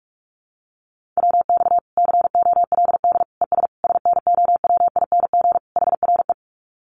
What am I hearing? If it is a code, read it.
Morse code, heard as a keyed tone, about 44 wpm, 700 Hz. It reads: WX COLD ES SNOWING HRE
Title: Common QSO Phrases 45wpm